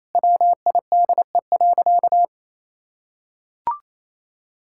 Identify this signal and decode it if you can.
Morse code, heard as a keyed tone, about 28 wpm, 700 Hz. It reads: WIDE.
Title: Sentences from Top 1000 Words - Review Part 6 - 28wpm